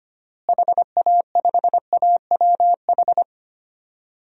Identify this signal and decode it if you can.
Morse code, heard as a keyed tone, about 25 wpm, 700 Hz. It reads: HA5AWH
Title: DXCC-100 Call Signs - 6 Characters - Repeated 3x - 25wpm